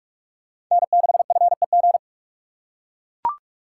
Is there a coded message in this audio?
Yes: Morse code from 0.7 to 2.0 s, heard as a keyed tone, about 44 wpm, 700 Hz. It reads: N6FEG